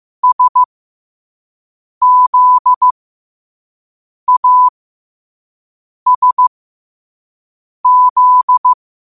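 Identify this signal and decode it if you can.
Morse code, heard as a keyed tone, about 15 wpm, spaced as for 5 wpm, 1000 Hz. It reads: SZASZ
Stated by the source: Morse code practice at 5 wpm.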